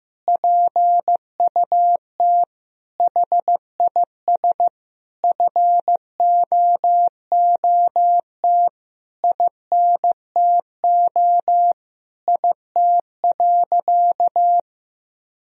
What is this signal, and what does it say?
Morse code, heard as a keyed tone, about 15 wpm, 700 Hz. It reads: PUT HIS FOOT INTO IT.
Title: Sentences from Top 1000 Words - Review Part 3 - 15wpm